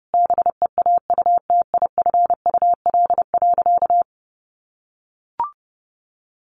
Morse code, heard as a keyed tone, about 30 wpm, 700 Hz. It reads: BEAUTIFUL.
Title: Sentences from Top 900 Words 30wpm